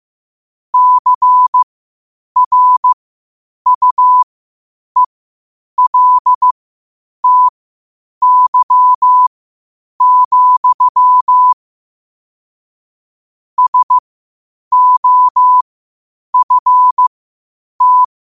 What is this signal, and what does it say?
Morse code, heard as a keyed tone, about 15 wpm, spaced as for 8 wpm, 1000 Hz. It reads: CRUELTY, SOFT
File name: qotdfiles / 8wpm